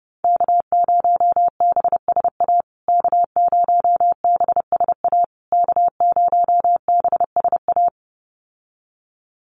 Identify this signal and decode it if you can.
Morse code, heard as a keyed tone, about 30 wpm, 700 Hz. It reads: K0BSA K0BSA K0BSA
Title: USA Call Signs Repeated 3x 30wpm